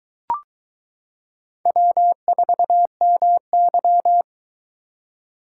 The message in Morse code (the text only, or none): W4MY